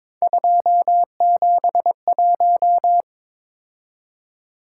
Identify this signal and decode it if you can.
Morse code, heard as a keyed tone, about 22 wpm, 700 Hz. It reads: 271
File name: practice / 22wpm / Three Letter-Number Combinations - 22wpm